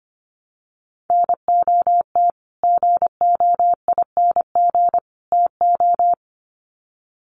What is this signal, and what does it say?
Morse code, heard as a keyed tone, about 25 wpm, 700 Hz. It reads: NOT GOING TO
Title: Sets of 3 Words 25wpm